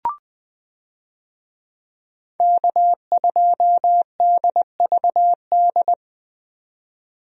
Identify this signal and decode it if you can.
Morse code, heard as a keyed tone, about 20 wpm, 700 Hz. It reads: K2DVD
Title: USA Call Signs - 5 Characters - 20wpm